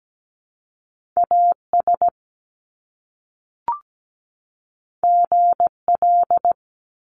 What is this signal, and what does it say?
Morse code, heard as a keyed tone, about 17 wpm, 700 Hz. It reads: AS GL
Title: Top Two Letter Combinations in Words 17wpm